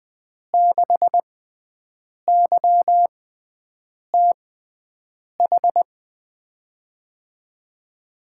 Morse code, heard as a keyed tone, about 20 wpm, spaced as for 7 wpm, 700 Hz. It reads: 6YTH
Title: USA Call Signs - 5 Characters - 6x Character Spacing - 20wpm